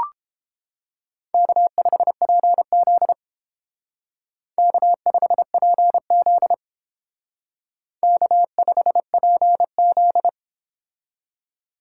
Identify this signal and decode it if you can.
Morse code, heard as a keyed tone, about 32 wpm, 700 Hz. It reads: K5PZ K5PZ K5PZ